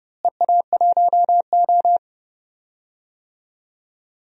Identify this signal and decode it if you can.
Morse code, heard as a keyed tone, about 30 wpm, 700 Hz. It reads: EA1O
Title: DXCC-100 Call Signs - 4 Characters - Repeated 2x - 30wpm